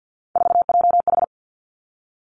Morse code, heard as a keyed tone, about 50 wpm, 700 Hz. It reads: 4JH
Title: Three Challenging Letter-Number Combinations 50wpm